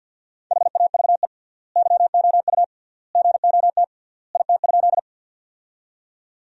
Morse code, heard as a keyed tone, about 50 wpm, 700 Hz. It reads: HAVE YOU GOT IT?
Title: 3x Word-Spacing - Sentences from Top 300 Words - 50wpm